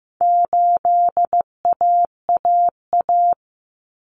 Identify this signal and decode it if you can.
Morse code, heard as a keyed tone, about 15 wpm, 700 Hz. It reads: 8AAA
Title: Common Cut Numbers - 4-Digit 15wpm